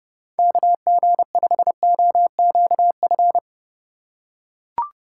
Morse code, heard as a keyed tone, about 30 wpm, 700 Hz. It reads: KG5OQF
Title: USA Call Signs - 6 Characters - 2x Character Spacing - 30wpm